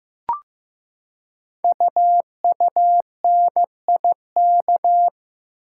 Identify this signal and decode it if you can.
Morse code, heard as a keyed tone, about 15 wpm, 690 Hz. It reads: UUNIK
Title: Code Groups - 5 Letters - 15wpm